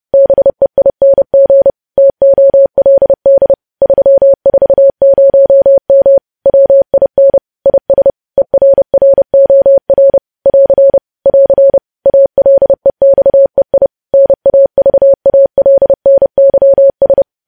Morse code, heard as a keyed tone, about 30 wpm, 550 Hz. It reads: BEING TOLD 340M WIN IS ERROR <AR> <AR> ALEXEI NAVALNYS
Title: News Headlines at 30